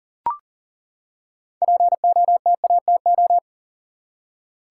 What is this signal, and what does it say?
Morse code, heard as a keyed tone, about 41 wpm, 700 Hz. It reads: POTATO
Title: Six Letter Words - 40wpm